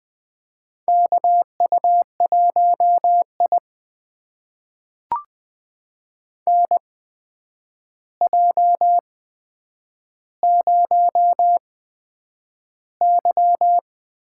Morse code, heard as a keyed tone, about 20 wpm, spaced as for 9 wpm, 700 Hz. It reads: KU1I NJ0Y
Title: USA Call Signs - 4 Characters - 8x Character Spacing - 20wpm